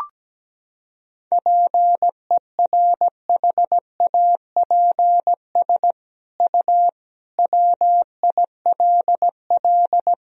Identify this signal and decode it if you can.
Morse code, heard as a keyed tone, about 17 wpm, 700 Hz. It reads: PERHAPS U WILL
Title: Sentences from Top 500 Words and Abbreviations 17wpm